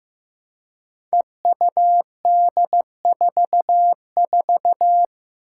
Morse code, heard as a keyed tone, about 15 wpm, 700 Hz. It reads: EUD44
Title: Five Challenging Letter-Number Combinations 15wpm